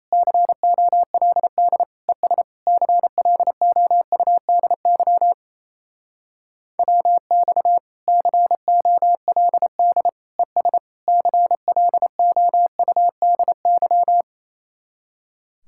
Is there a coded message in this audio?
Yes: Morse code from 0.1 to 14.2 s, heard as a keyed tone, about 37 wpm, 700 Hz. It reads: COLD ES CLOUDY WX COLD ES CLOUDY